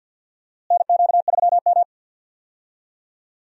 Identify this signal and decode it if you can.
Morse code, heard as a keyed tone, about 47 wpm, 690 Hz. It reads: NQ3M
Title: USA Call Signs - 4 Characters - Repeated 2x - 50wpm